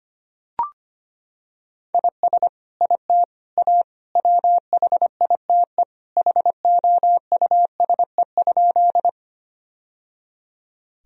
Morse code, heard as a keyed tone, about 25 wpm, 700 Hz. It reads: IS IT A WHITE HOUSE?
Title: Sentences from Top 1000 Words - Review Part 1 - 25wpm